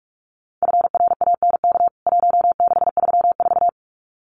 Morse code, heard as a keyed tone, about 44 wpm, 700 Hz. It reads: FRANK 1634
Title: CWT Exchange 45wpm